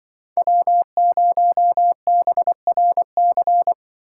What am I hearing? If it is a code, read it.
Morse code, heard as a keyed tone, about 24 wpm, 700 Hz. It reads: W0BRC
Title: Speed Racing - USA 1x3 Call Signs 20wpm